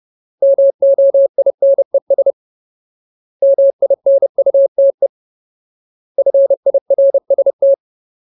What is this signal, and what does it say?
Morse code, heard as a keyed tone, about 30 wpm, 550 Hz. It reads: MOINES MINUTE FIRST
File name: Ultimate Word List / 30WPM Ultimate